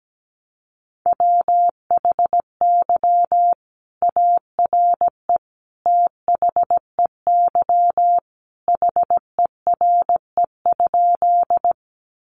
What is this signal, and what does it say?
Morse code, heard as a keyed tone, about 17 wpm, 700 Hz. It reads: WHY ARE THEY HERE?
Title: Sentences from Top 200 Words 17wpm